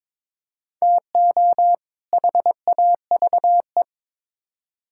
Morse code, heard as a keyed tone, about 22 wpm, 700 Hz. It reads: TO HAVE
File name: practice / 22wpm / Build A Phrase - 5-Word Phrases - 22wpm